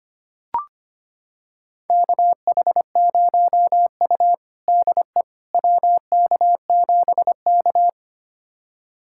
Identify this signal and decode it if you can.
Morse code, heard as a keyed tone, about 25 wpm, 700 Hz. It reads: KH0U DE WK7K
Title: USA Callsign DE Callsign - 4 Characters - 25wpm